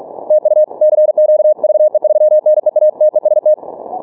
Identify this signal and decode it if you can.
Morse code, heard as a keyed tone, about 43 wpm, 610 Hz. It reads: TU CQ VE3NEA TEST
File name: training / morserunner / Example